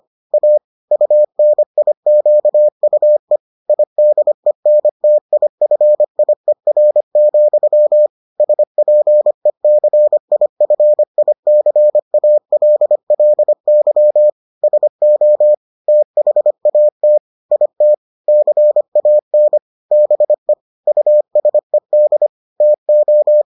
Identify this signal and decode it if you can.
Morse code, heard as a keyed tone, about 25 wpm, 600 Hz. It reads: A UNIQUE IDENTIFIER, SPECIFICALLY SO THAT IT CAN BE USED TO